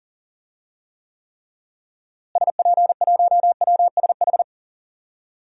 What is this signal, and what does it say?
Morse code, heard as a keyed tone, about 40 wpm, 700 Hz. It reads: SP1WSH
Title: DXCC-100 Call Signs - 6 Characters - Repeated 3x - 40wpm